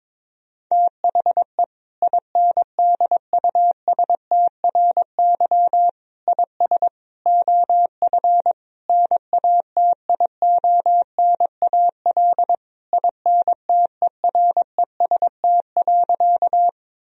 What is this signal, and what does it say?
Morse code, heard as a keyed tone, about 21 wpm, 700 Hz. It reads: THE INDUSTRY IS OF NATIONAL INTEREST.
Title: Sentences from Top 1000 Words - Review Part 4 - 22wpm